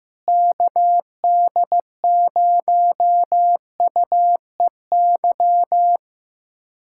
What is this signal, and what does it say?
Morse code, heard as a keyed tone, about 15 wpm, 700 Hz. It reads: KD0UEY
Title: USA Callsign DE Callsign 15wpm